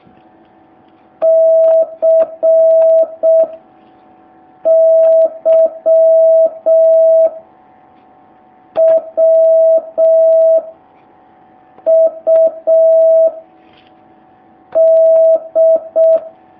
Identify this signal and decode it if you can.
Morse code, heard as a keyed tone, about 6 wpm, 650 Hz. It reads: C Y W U D
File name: jh8cbh / 2019cw